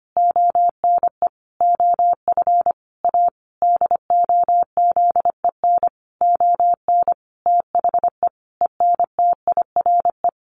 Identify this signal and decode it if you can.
Morse code, heard as a keyed tone, about 25 wpm, 700 Hz. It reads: ONE OF A DOZEN ON THE ENTIRE
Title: A Sound Of Thunder 25wpm